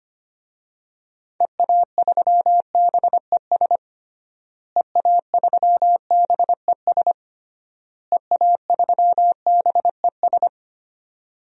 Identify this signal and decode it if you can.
Morse code, heard as a keyed tone, about 25 wpm, 700 Hz. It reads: EA3BES EA3BES EA3BES
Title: DXCC-100 Call Signs - 6 Characters - Repeated 3x - 25wpm